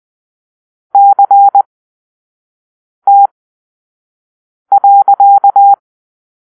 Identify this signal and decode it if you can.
Morse code, heard as a keyed tone, about 20 wpm, spaced as for 5 wpm, 800 Hz. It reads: CT.
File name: arrl / Grp-6 Drill-3